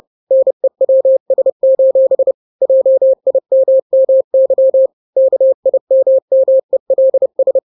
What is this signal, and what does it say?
Morse code, heard as a keyed tone, about 30 wpm, 530 Hz. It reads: NEWS: JIMMY KIMMELS
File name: daily / 29wpm